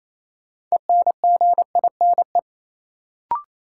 Morse code, heard as a keyed tone, about 28 wpm, 700 Hz. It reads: ENGINE
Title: Sentences from Top 1500 Words - New Word Drill - 28wpm